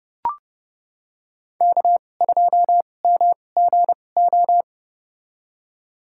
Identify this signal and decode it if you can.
Morse code, heard as a keyed tone, about 30 wpm, spaced as for 22 wpm, 700 Hz. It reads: K2MGO
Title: USA Call Signs - 5 Characters - 2x Character Spacing - 30wpm